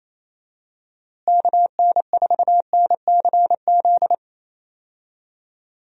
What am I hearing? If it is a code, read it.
Morse code, heard as a keyed tone, about 28 wpm, 700 Hz. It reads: KN4NCZ
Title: USA Call Signs - 6 Characters - Repeated 3x - 28wpm